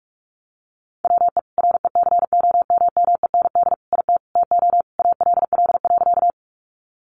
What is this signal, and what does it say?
Morse code, heard as a keyed tone, about 44 wpm, 700 Hz. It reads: WE RECOMMEND IT TO ALL.
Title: Sentences from Top 1200 Words 45wpm